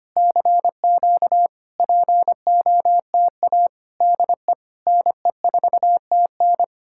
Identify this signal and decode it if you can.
Morse code, heard as a keyed tone, about 30 wpm, 700 Hz. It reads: CQ POTA DE NE4TN
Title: Speed-Racing Common QSO Phrases Encore - 25wpm